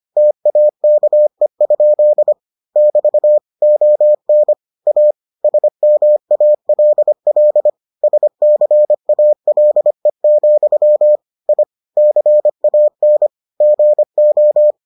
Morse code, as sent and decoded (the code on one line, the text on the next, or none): - .- -.- . ..--.. / -...- / --- -. / .- / ... -- .- .-.. .-.. / ... -.-. .- .-.. --..-- / .. / -.-. .- -. / --. ---
TAKE? <BT> ON A SMALL SCALE, I CAN GO